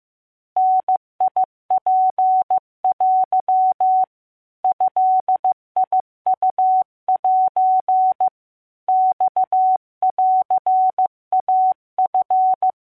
Morse code (Japanese Wabun code), heard as a keyed tone, about 15 wpm, 750 Hz. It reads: ダツテ ドウセ マンイチ